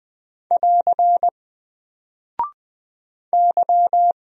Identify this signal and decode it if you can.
Morse code, heard as a keyed tone, about 21 wpm, 700 Hz. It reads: <AR> Y
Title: Single Character (Letter:Number:Punctuation:Prosign) Alt 20wpm